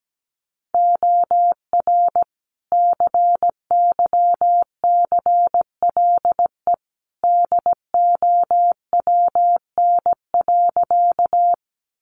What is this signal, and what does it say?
Morse code, heard as a keyed tone, about 17 wpm, 700 Hz. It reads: OR CYCLE DOWN.
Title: Sentences from Top 900 Words 17wpm